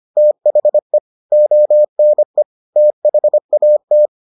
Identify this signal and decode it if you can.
Morse code, heard as a keyed tone, about 25 wpm, 600 Hz. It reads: THE ONE THAT